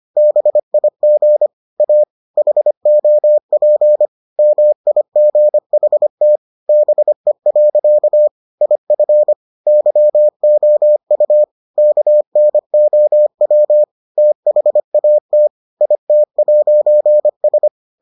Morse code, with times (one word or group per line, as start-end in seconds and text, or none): 0.2-1.5 BIG
1.8-2.0 A
2.4-4.1 HOP
4.4-6.4 MIGHT
6.7-8.3 BE.
8.6-9.3 IF
9.7-11.4 YOU
11.8-13.8 KNOW
14.2-15.5 THAT
15.8-17.7 IT'S